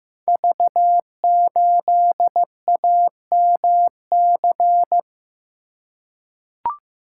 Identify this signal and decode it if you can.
Morse code, heard as a keyed tone, about 15 wpm, 700 Hz. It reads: V8AMC
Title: Code Groups - 5 Letter:Numbers - 15wpm